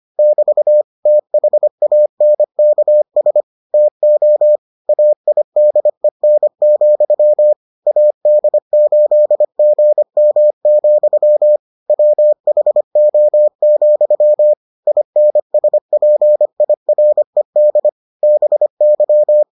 Morse code, heard as a keyed tone, about 25 wpm, 600 Hz. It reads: <BT> THANKS TO AIDEN, AD8GM, WHO, INSPIRED BY